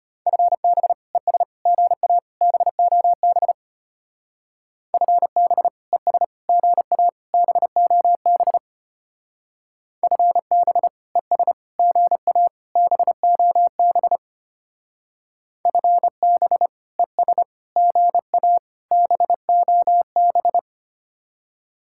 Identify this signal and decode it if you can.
Morse code, heard as a keyed tone, about 30 wpm, 700 Hz. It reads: FB ES GA BOB FB ES GA BOB FB ES GA BOB FB ES GA BOB
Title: Speed-Racing Common QSO Phrases - 25wpm